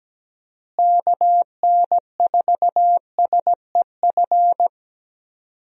Morse code, heard as a keyed tone, about 17 wpm, 700 Hz. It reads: KN4SEF
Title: USA Call Signs - 6 Characters - Repeated 2x - 17wpm